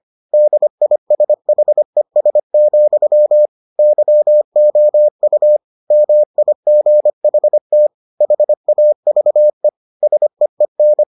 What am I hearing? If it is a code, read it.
Morse code, heard as a keyed tone, about 25 wpm, 600 Hz. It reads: DISHES, YOU MIGHT HAVE SEEN